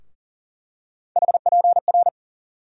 Morse code, heard as a keyed tone, about 40 wpm, 700 Hz. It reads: HPR
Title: USA Call Sign Postfixes - 3 Letters 40wpm